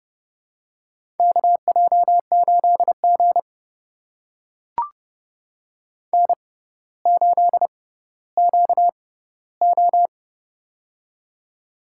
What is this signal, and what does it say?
Morse code, heard as a keyed tone, about 30 wpm, spaced as for 15 wpm, 700 Hz. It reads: KJ8G N8QO
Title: USA Call Signs - 4 Characters - 6x Character Spacing - 30wpm